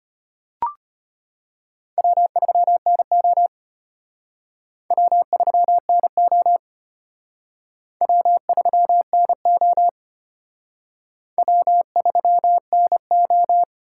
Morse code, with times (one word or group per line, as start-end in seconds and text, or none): 2.0-3.5 W3NO
4.9-6.6 W3NO
8.0-9.9 W3NO
11.4-13.6 W3NO